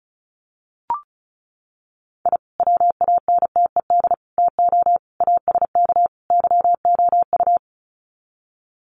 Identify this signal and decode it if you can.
Morse code, heard as a keyed tone, about 34 wpm, 700 Hz. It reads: I WANTED TO ASK YOU
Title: Sets of 5 Words Encore2 35wpm